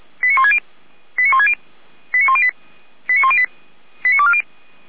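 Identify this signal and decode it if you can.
Five-tone selective call (ZVEI), heard as quick runs of five tones, tones of about 70 ms. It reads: 89160 89160 89108 89108 89240
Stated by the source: ZVEI five-tone selective call received on SDR.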